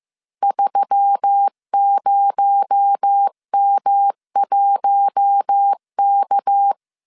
Morse code, heard as a keyed tone, about 15 wpm, 800 Hz. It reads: 30M1K